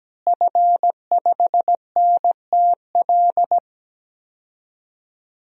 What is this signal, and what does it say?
Morse code, heard as a keyed tone, about 17 wpm, 700 Hz. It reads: F5NTL
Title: DXCC-100 Call Signs - 5 Characters - Repeated 2x - 17wpm